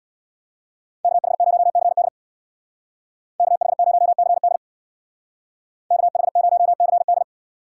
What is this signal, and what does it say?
Morse code, heard as a keyed tone, about 64 wpm, 700 Hz. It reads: KH0CD KH0CD KH0CD